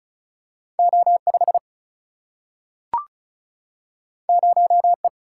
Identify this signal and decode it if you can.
Morse code, heard as a keyed tone, about 36 wpm, 700 Hz. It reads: O5 0E
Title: Single-Letter Single-Number 35wpm